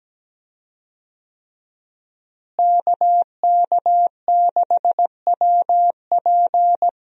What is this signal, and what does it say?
Morse code, heard as a keyed tone, about 17 wpm, 700 Hz. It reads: KK6WP
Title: USA Call Signs - 5 Characters - Repeated 3x - 17wpm